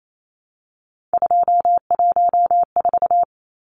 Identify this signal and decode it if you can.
Morse code, heard as a keyed tone, about 28 wpm, 700 Hz. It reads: 214
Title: Numbers - 3-Digit 28wpm